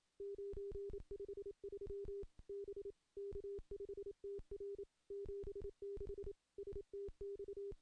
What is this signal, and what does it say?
Morse code, heard as a keyed tone, about 27 wpm, 400 Hz. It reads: 953 B K5TR 76 STX
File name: contests / CW-SS-k5tr- -1500